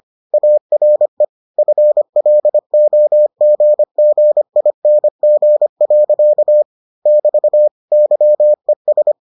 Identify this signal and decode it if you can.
Morse code, heard as a keyed tone, about 25 wpm, 600 Hz. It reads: ARE FLOGGING. <BT> YES